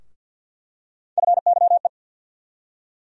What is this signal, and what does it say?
Morse code, heard as a keyed tone, about 47 wpm, 700 Hz. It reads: FYE
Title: USA Call Sign Postfixes - 3 Letters 50wpm